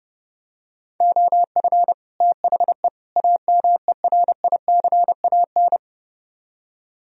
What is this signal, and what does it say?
Morse code, heard as a keyed tone, about 30 wpm, 700 Hz. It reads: OF THE AMERICAN